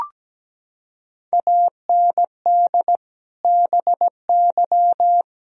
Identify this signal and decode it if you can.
Morse code, heard as a keyed tone, about 17 wpm, 700 Hz. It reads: AND BY
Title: Sentences from Top 1000 Words - Review Part 4 - 17wpm